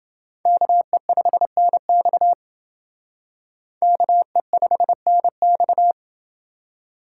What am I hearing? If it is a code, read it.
Morse code, heard as a keyed tone, about 28 wpm, 700 Hz. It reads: KE5NX KE5NX